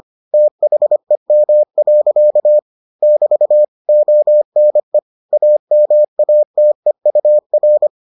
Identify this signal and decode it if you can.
Morse code, heard as a keyed tone, about 25 wpm, 600 Hz. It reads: THEM. <BT> ONE AMATEUR